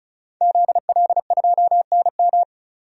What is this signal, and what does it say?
Morse code, heard as a keyed tone, about 36 wpm, 700 Hz. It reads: ZL2NM
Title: DXCC-100 Call Signs - 5 Characters - 35wpm